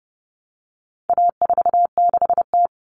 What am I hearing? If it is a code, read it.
Morse code, heard as a keyed tone, about 30 wpm, 700 Hz. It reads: A46T
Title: Common Cut Numbers - 4-Digit 30wpm